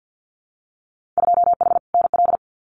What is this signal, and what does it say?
Morse code, heard as a keyed tone, about 50 wpm, 700 Hz. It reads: 2H NL
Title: Winter Field Day Exchanges Encore 50wpm